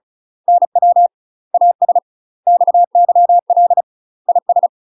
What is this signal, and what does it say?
Morse code, heard as a keyed tone, about 34 wpm, 700 Hz. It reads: NW AS XYL IS
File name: UK MASTER QSOs TRAINING / 35WPM / UK MASTER QSOs 0004